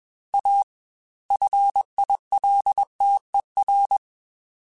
Morse code, heard as a keyed tone, about 21 wpm, 800 Hz. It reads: A FILTER
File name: Technical Subjects / Technical Subjects-mp3s / Sw Pwr Sup @20